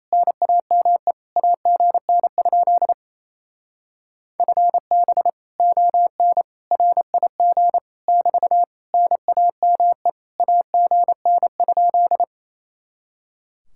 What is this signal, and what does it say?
Morse code, heard as a keyed tone, about 28 wpm, 700 Hz. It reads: NAME AGN? FB ON RIG <BT> NAME AGN?
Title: Speed-Racing Common QSO Phrases Encore - 28wpm